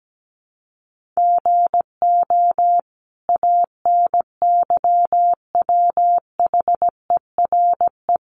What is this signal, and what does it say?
Morse code, heard as a keyed tone, about 17 wpm, 700 Hz. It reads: GO ANYWHERE
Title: Sentences from Top 1400 Words 17wpm